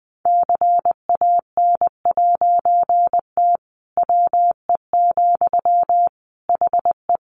Morse code, heard as a keyed tone, about 20 wpm, 700 Hz. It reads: CAN'T WE, HE
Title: A Sound Of Thunder 20wpm